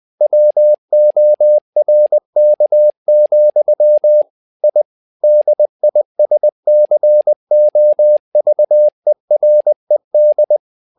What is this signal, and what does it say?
Morse code, heard as a keyed tone, about 20 wpm, 600 Hz. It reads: WORK, I DISCOVERED
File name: tower trouble 20wpm